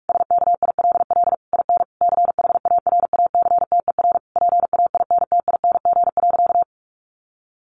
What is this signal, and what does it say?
Morse code, heard as a keyed tone, about 44 wpm, 700 Hz. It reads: SKILL IN CHARACTER PAINTING.
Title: Sentences from Top 1200 Words 45wpm